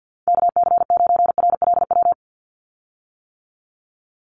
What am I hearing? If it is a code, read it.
Morse code, heard as a keyed tone, about 50 wpm, 700 Hz. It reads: KC9RLW
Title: USA Call Signs 50wpm